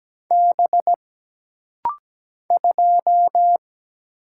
Morse code, heard as a keyed tone, about 17 wpm, 700 Hz. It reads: B 2
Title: Single Letter-Number - Rapid-Fire 17wpm